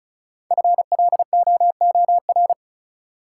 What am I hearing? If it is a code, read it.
Morse code, heard as a keyed tone, about 36 wpm, 700 Hz. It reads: FLOOR
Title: Sets of 3 Words - New Word Drill - 35wpm